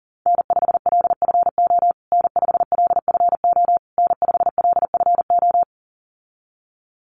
Morse code, heard as a keyed tone, about 40 wpm, 700 Hz. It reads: N5LFO N5LFO N5LFO
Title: USA Call Signs Repeated 3x 40wpm